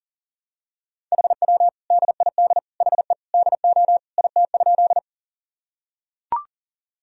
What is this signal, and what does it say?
Morse code, heard as a keyed tone, about 40 wpm, 700 Hz. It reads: HW DID HE DO IT?
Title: Sentences from Top 100 Words and Abbreviations 40wpm